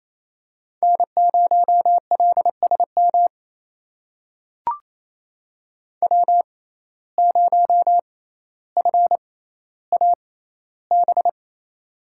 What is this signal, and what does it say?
Morse code, heard as a keyed tone, about 28 wpm, spaced as for 14 wpm, 700 Hz. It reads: N0LSM W0FAB
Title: USA Call Signs - 5 Characters - 6x Character Spacing - 28wpm